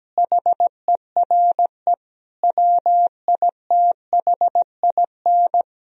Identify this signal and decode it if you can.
Morse code, heard as a keyed tone, about 17 wpm, 700 Hz. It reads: HERE WITHIN
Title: Sentences from Top 1000 Words - Review Part 4 - 17wpm